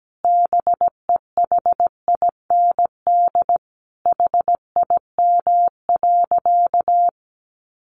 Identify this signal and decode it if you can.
Morse code, heard as a keyed tone, about 17 wpm, 700 Hz. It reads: BEHIND HIM.
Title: Sentences from Top 700 Words 17wpm